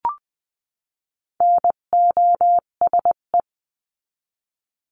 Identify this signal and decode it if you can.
Morse code, heard as a keyed tone, about 20 wpm, 700 Hz. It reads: NOSE
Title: 258 Lesson 42c - Words 16-20wpm